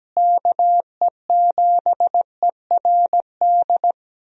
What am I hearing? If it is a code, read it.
Morse code, heard as a keyed tone, about 17 wpm, 700 Hz. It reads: KE7ERD
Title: USA Call Signs - 6 Characters - Repeated 2x - 17wpm